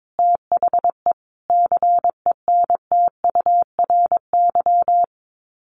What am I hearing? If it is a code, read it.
Morse code, heard as a keyed tone, about 22 wpm, 700 Hz. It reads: THE CENTURY
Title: Sets of 5 Words 22wpm